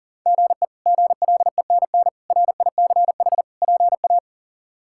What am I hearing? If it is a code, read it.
Morse code, heard as a keyed tone, about 40 wpm, 700 Hz. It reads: GE GLENN RICH PA
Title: K1USN SST QSO Practice - 40wpm